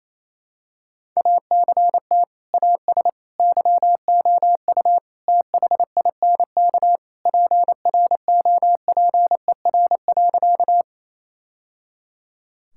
Morse code, heard as a keyed tone, about 28 wpm, 700 Hz. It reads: ACT AS YOU THINK PROPER.
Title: Sentences from Top 1000 Words - Review Part 5 - 28wpm